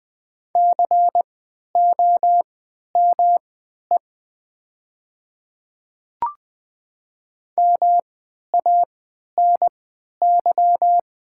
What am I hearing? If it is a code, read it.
Morse code, heard as a keyed tone, about 20 wpm, spaced as for 11 wpm, 700 Hz. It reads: COME MANY
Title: 3x Character Spacing - Top 100 Words - 20wpm